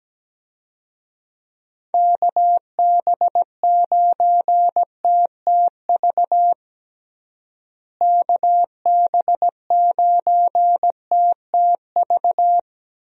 Morse code, heard as a keyed tone, about 17 wpm, 700 Hz. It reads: KB9TTV KB9TTV
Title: USA Call Signs - 6 Characters - Repeated 3x - 17wpm